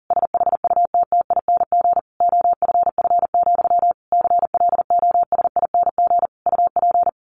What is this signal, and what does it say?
Morse code, heard as a keyed tone, about 40 wpm, 700 Hz. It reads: SHUTTING OFF, CLOSING UP
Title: A Sound Of Thunder 40wpm